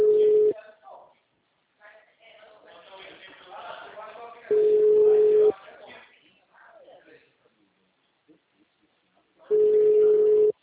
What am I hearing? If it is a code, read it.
Ringback tone (Europe, 425 Hz), with speech.
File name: out- -2004- - - .40033